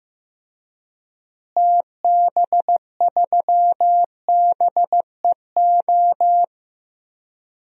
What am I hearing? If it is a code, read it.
Morse code, heard as a keyed tone, about 15 wpm, 700 Hz. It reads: TB3BEO
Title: DXCC-100 Call Signs - 6 Characters - Repeated 2x - 15wpm